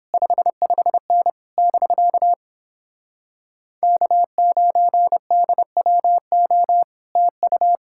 Morse code, heard as a keyed tone, about 29 wpm, 700 Hz. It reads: 55N <BK> K9DWO TU